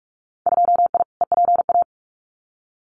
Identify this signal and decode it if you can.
Morse code, heard as a keyed tone, about 46 wpm, 690 Hz. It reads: 2I EPA
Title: Winter Field Day Exchanges Encore 45wpm